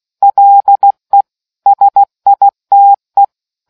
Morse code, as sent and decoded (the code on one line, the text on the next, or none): .-.. / ... .. - .
LE SITE